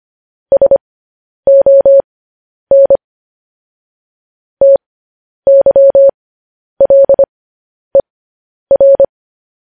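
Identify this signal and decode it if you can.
Morse code, heard as a keyed tone, about 25 wpm, spaced as for 10 wpm, 550 Hz. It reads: SON TYLER
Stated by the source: Morse news headlines at 10.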